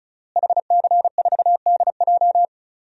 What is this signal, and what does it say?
Morse code, heard as a keyed tone, about 34 wpm, 700 Hz. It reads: HC4DJ
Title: DXCC-100 Call Signs - 5 Characters - Repeated 3x - 35wpm